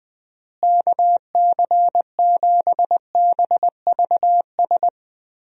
Morse code, heard as a keyed tone, about 21 wpm, 700 Hz. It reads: KC7BVS
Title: USA Call Signs - 6 Characters - Repeated 2x - 20wpm